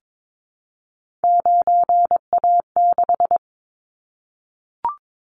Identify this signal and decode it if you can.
Morse code, heard as a keyed tone, about 22 wpm, 700 Hz. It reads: 9A6
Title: Callsign Prefixes for Top 50 DXCC - 22wpm